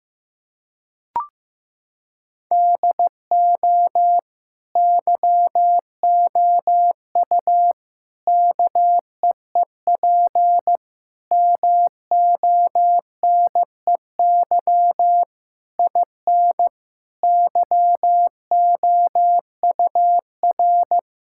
Morse code, heard as a keyed tone, about 15 wpm, 700 Hz. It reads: DO YOU KEEP MONEY IN YOUR
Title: Sentences from Top 1000 Words - Review Part 6 - 15wpm